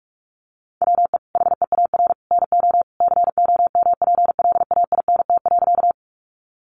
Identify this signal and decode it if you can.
Morse code, heard as a keyed tone, about 46 wpm, 700 Hz. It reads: WE HEAR NO COMPLAINT.
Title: Sentences from Top 1400 Words 45wpm